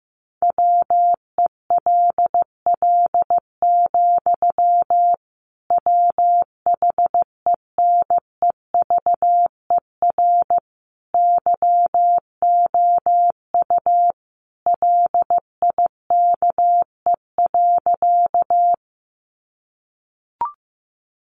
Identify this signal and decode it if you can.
Morse code, heard as a keyed tone, about 15 wpm, 700 Hz. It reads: WELL, WHENEVER YOU LIKE.
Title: Sentences from Top 1200 Words 15wpm